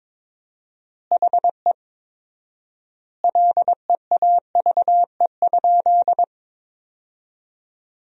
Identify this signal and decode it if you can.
Morse code, heard as a keyed tone, about 22 wpm, 690 Hz. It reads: HE LEAVE?
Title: 4x Word-Spacing - Sentences from Top 400 Words - 22wpm